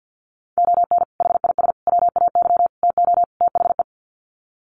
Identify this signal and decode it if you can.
Morse code, heard as a keyed tone, about 50 wpm, 700 Hz. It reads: ON HIS WAY TO THE
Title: Sets of 5 Words Encore2 50wpm